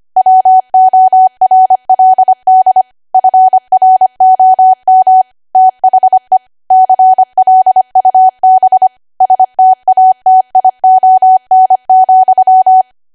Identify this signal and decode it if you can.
Morse code, heard as a keyed tone, about 25 wpm, 750 Hz. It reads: WORLD FROM THE CLUB STATION,
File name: Archive / 25 WPM / 25WPM